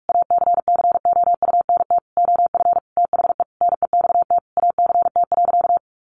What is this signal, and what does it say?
Morse code, heard as a keyed tone, about 44 wpm, 700 Hz. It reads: ACCOUNT OF THE NEXT ACT.
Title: Sentences from Top 900 Words 45wpm